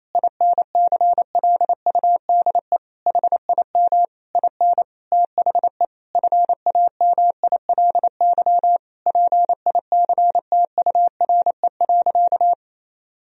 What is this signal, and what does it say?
Morse code, heard as a keyed tone, about 28 wpm, 700 Hz. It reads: INCLUDE HIM IN THE FAMILY PICTURE.
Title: Sentences from Top 1000 Words - Review Part 3 - 28wpm